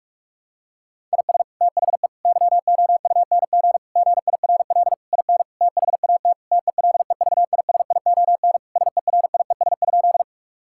Morse code, heard as a keyed tone, about 46 wpm, 700 Hz. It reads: IS THE YOUNG GIRL IN THAT TELEVISION SERIES?
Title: Sentences from Top 1000 Words - Review Part 5 - 45wpm